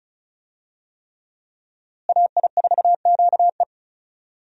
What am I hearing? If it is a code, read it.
Morse code, heard as a keyed tone, about 34 wpm, 700 Hz. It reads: AI4QE
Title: USA Call Signs - 5 Characters - Repeated 3x - 35wpm